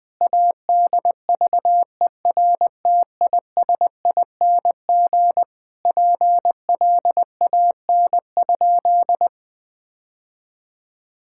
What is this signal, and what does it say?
Morse code, heard as a keyed tone, about 21 wpm, 700 Hz. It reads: ADVERTISING PLAN?
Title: Sentences from Top 1000 Words - Review Part 6 - 20wpm